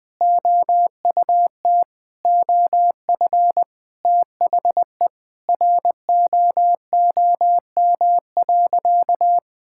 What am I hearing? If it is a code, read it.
Morse code, heard as a keyed tone, about 20 wpm, 700 Hz. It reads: OUT OF THE ROOM.
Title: Sentences with 3-Word Phrases - Spoken Beginning - 20wpm